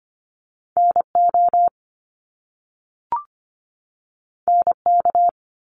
Morse code, heard as a keyed tone, about 25 wpm, 700 Hz. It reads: NO NK
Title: Top Two Letter Combinations in Words 25wpm